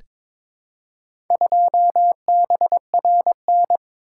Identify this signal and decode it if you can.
Morse code, heard as a keyed tone, about 22 wpm, 700 Hz. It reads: 2BRN
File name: practice / 22wpm / Four Letter-Number Combinations - 22wpm